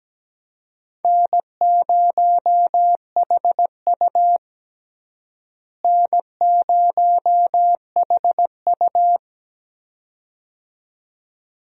Morse code, heard as a keyed tone, about 17 wpm, 700 Hz. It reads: N0HU N0HU
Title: USA Call Signs - 4 Characters - Repeated 2x - 17wpm